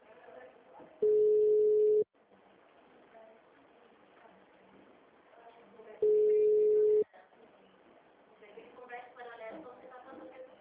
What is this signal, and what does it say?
Ringback tone (Europe, 425 Hz), with speech.
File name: out- -2010- - - .27626